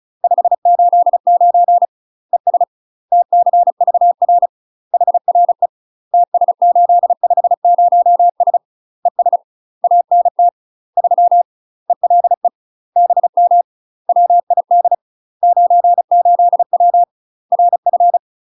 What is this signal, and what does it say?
Morse code, heard as a keyed tone, about 36 wpm, 700 Hz. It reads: 589 ES TCVR HRE TS850S ES ANT 3 ELE BM WID 98W RF
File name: UK MASTER QSOs TRAINING / 35WPM / UK MASTER QSOs 0016